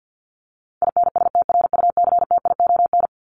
Morse code, heard as a keyed tone, about 50 wpm, 700 Hz. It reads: INSTRUCTION
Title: Top 1751-2000 Words 50wpm